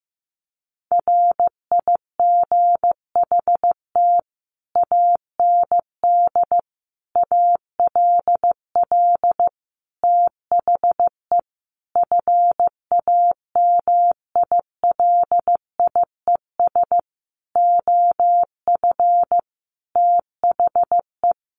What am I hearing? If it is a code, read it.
Morse code, heard as a keyed tone, about 15 wpm, 700 Hz. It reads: RIGHT AND ALL THE FAMILIES OF THE
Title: A Sound Of Thunder 15wpm